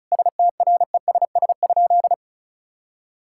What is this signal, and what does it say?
Morse code, heard as a keyed tone, about 36 wpm, 700 Hz. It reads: STRESS?
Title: Sentences from Top 1000 Words - Review Part 6 - 35wpm